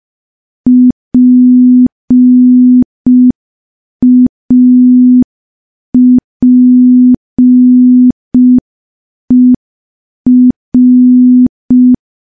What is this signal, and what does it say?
Morse code, heard as a keyed tone, about 5 wpm, 250 Hz. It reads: PAPER
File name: morse / morse-training-05WPM